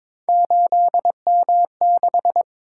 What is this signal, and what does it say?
Morse code, heard as a keyed tone, about 22 wpm, 700 Hz. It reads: 8M6
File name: practice / 22wpm / Three Letter-Number Combinations - 22wpm